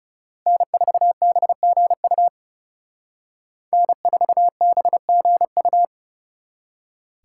Morse code, heard as a keyed tone, about 36 wpm, 700 Hz. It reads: N4BGU N4BGU
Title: Speed Racing - USA 1x3 Call Signs 30wpm